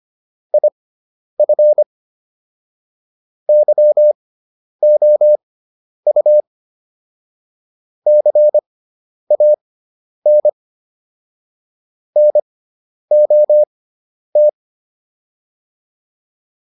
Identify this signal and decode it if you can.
Morse code, heard as a keyed tone, about 25 wpm, spaced as for 10 wpm, 600 Hz. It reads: IF YOU CAN NOT
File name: WD304 10